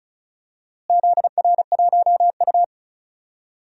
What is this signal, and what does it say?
Morse code, heard as a keyed tone, about 34 wpm, 700 Hz. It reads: ZR1U
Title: DXCC-100 Call Signs - 4 Characters - Repeated 2x - 35wpm